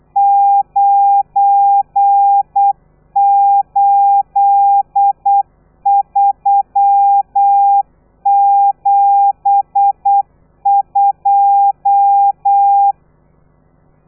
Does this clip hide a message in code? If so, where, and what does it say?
Yes: Morse code from 0.2 to 12.9 s, heard as a keyed tone, about 8 wpm, 800 Hz. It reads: 98372